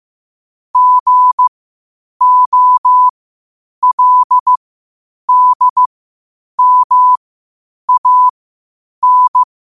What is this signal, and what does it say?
Morse code, heard as a keyed tone, about 15 wpm, spaced as for 8 wpm, 1000 Hz. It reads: GOLDMAN
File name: qotdfiles / 8wpm